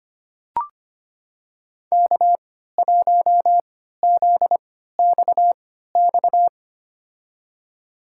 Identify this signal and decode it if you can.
Morse code, heard as a keyed tone, about 25 wpm, spaced as for 14 wpm, 700 Hz. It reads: K1ZXX
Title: USA Call Signs - 5 Characters - 3x Character Spacing - 25wpm